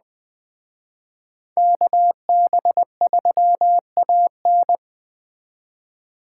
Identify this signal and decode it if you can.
Morse code, heard as a keyed tone, about 21 wpm, 700 Hz. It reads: KB3AN